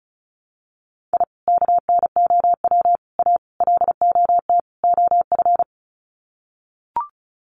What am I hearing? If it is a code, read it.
Morse code, heard as a keyed tone, about 34 wpm, 700 Hz. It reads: I KNOW A LOT OF
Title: Sets of 5 Words Encore 35wpm